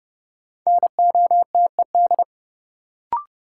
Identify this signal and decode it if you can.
Morse code, heard as a keyed tone, about 30 wpm, 700 Hz. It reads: NOTED
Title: Sets of 5 Words - New Word Drill - 30wpm